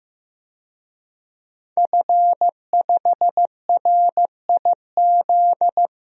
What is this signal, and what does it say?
Morse code, heard as a keyed tone, about 15 wpm, 700 Hz. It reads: F5RIZ
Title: DXCC-100 Call Signs - 5 Characters - Repeated 3x - 15wpm